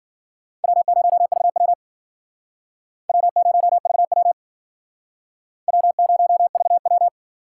Morse code, heard as a keyed tone, about 50 wpm, 700 Hz. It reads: W0VW W0VW W0VW